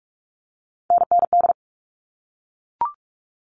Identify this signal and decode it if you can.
Morse code, heard as a keyed tone, about 44 wpm, 700 Hz. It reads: NND
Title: All Cut Numbers - 3-Digit 45wpm